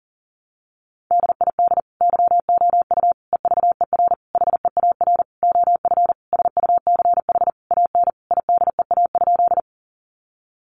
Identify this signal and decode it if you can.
Morse code, heard as a keyed tone, about 40 wpm, 700 Hz. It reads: DID YOU EVER HEAR OF SUCH AN IDEA?
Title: Sentences from Top 400 Words 40wpm